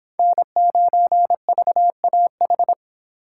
Morse code, heard as a keyed tone, about 24 wpm, 700 Hz. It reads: N9VAH